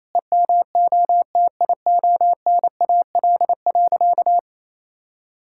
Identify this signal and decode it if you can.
Morse code, heard as a keyed tone, about 28 wpm, 700 Hz. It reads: EMOTIONAL.
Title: Sentences from Top 1000 Words - Review Part 5 - 28wpm